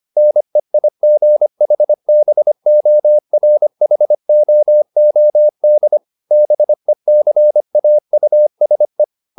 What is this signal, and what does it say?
Morse code, heard as a keyed tone, about 25 wpm, 600 Hz. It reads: NEIGHBORHOOD BECAUSE